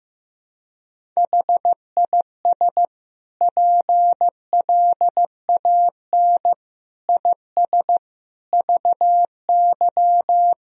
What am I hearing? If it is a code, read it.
Morse code, heard as a keyed tone, about 15 wpm, 700 Hz. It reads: HIS PLAN IS VY
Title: Sentences from Top 500 Words and Abbreviations 15wpm